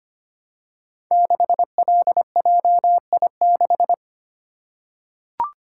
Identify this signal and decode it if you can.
Morse code, heard as a keyed tone, about 25 wpm, 700 Hz. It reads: 6LJI6
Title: Code Groups - 5 Letter:Numbers - 25wpm